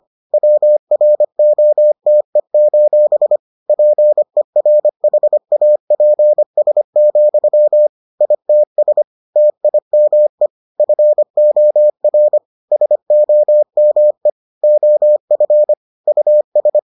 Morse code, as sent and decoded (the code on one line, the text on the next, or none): .-- .-. --- - . ---... / .--. .-. .... .- .--. ... --..-- / .. - ... / - .. -- . / ..-. --- .-. / ... --- -- . / --- ..-. / ..- ...
WROTE: PERHAPS, ITS TIME FOR SOME OF US